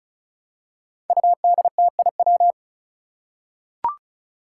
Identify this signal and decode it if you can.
Morse code, heard as a keyed tone, about 34 wpm, 700 Hz. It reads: UDTIW
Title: Five Challenging Letter-Number Combinations 35wpm